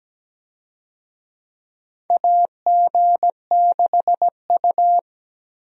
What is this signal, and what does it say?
Morse code, heard as a keyed tone, about 17 wpm, 700 Hz. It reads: AG6U
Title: USA Call Signs - 4 Characters - Repeated 3x - 17wpm